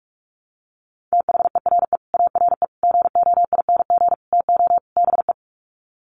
Morse code, heard as a keyed tone, about 44 wpm, 700 Hz. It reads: THERE ARE GOING TO BE
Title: Sets of 5 Words 45wpm